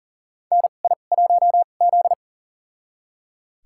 Morse code, heard as a keyed tone, about 40 wpm, spaced as for 29 wpm, 700 Hz. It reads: NI1Z
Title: USA Call Signs - 4 Characters - 2x Character Spacing - 40wpm